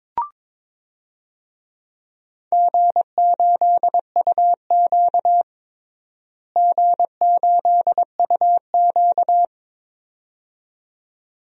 Morse code, heard as a keyed tone, about 22 wpm, 700 Hz. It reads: G8UQ G8UQ
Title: DXCC-100 Call Signs - 4 Characters - Repeated 2x - 22wpm